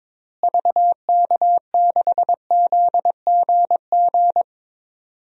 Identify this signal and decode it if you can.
Morse code, heard as a keyed tone, about 22 wpm, 700 Hz. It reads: VK6ZGG
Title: DXCC-100 Call Signs - 6 Characters - Repeated 2x - 22wpm